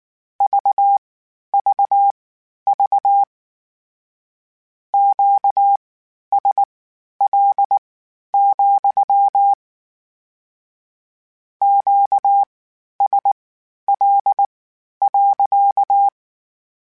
Morse code, heard as a keyed tone, about 19 wpm, spaced as for 11 wpm, 800 Hz. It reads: VVV QSL, QSL.